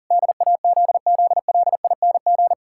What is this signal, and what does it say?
Morse code, heard as a keyed tone, about 40 wpm, 700 Hz. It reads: DAZZLING
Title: Pangram Challenge v2 - 40wpm